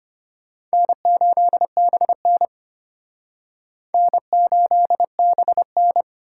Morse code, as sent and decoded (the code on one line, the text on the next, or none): -. ---.. -... -. / -. ---.. -... -.
N8BN N8BN